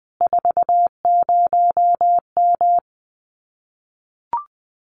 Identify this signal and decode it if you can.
Morse code, heard as a keyed tone, about 20 wpm, 700 Hz. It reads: 40M
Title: Common QSO Phrases Encore 20wpm